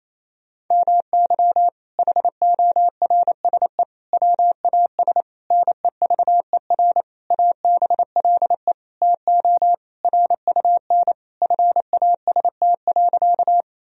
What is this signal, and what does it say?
Morse code, heard as a keyed tone, about 28 wpm, 700 Hz. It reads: MY HORSE WAS NEVER ABLE TO RUN FAST.
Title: Sentences from Top 1000 Words - Review Part 2 - 28wpm